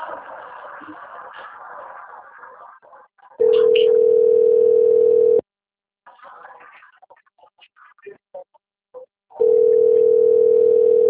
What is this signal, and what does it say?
Ringback tone (North America), 440 + 480 Hz.